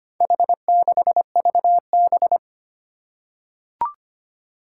Morse code, heard as a keyed tone, about 25 wpm, 700 Hz. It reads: H6VB
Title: Four Challenging Letter-Number Combinations 25wpm